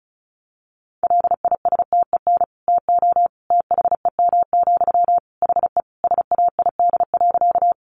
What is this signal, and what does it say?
Morse code, heard as a keyed tone, about 34 wpm, 700 Hz. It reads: LISTEN TO THEM, HE SAID.
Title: Sentences from Top 400 Words 35wpm